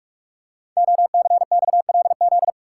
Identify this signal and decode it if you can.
Morse code, heard as a keyed tone, about 44 wpm, 700 Hz. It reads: OCXLZ
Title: Code Groups - 5 Letters - 45wpm